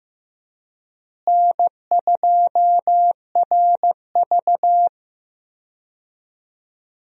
Morse code, heard as a keyed tone, about 15 wpm, 700 Hz. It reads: N2RV
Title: USA Call Signs - 4 Characters - Repeated 3x - 15wpm